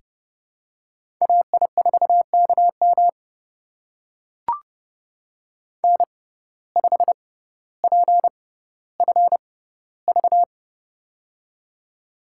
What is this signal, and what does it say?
Morse code, heard as a keyed tone, about 30 wpm, spaced as for 15 wpm, 700 Hz. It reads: AI4KM N5PFV
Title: USA Call Signs - 5 Characters - 6x Character Spacing - 30wpm